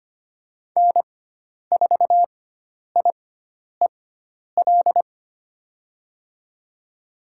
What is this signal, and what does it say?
Morse code, heard as a keyed tone, about 25 wpm, spaced as for 10 wpm, 700 Hz. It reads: N4IEL
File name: Lesson 16g - Letter c - Call Signs 25wpm